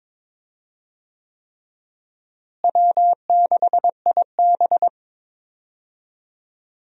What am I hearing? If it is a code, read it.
Morse code, heard as a keyed tone, about 22 wpm, 700 Hz. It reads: W6IB
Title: USA Call Signs - 4 Characters - Repeated 3x - 22wpm